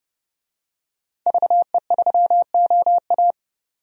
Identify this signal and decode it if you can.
Morse code, heard as a keyed tone, about 31 wpm, 700 Hz. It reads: VE3OA
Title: DXCC-100 Call Signs - 5 Characters - 30wpm